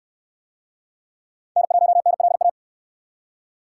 Morse code, heard as a keyed tone, about 66 wpm, 700 Hz. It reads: N1NLA